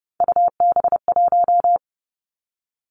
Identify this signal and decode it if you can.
Morse code, heard as a keyed tone, about 30 wpm, 700 Hz. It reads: UB1
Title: Three Challenging Letter-Number Combinations 30wpm